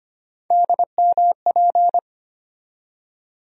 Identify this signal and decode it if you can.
Morse code, heard as a keyed tone, about 25 wpm, 700 Hz. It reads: DMP
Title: USA Call Sign Postfixes - 3 Letters 25wpm